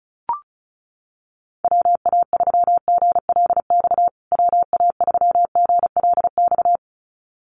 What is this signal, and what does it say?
Morse code, heard as a keyed tone, about 34 wpm, 700 Hz. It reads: WA3GLX WA3GLX
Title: USA Call Signs Repeated 2x 35wpm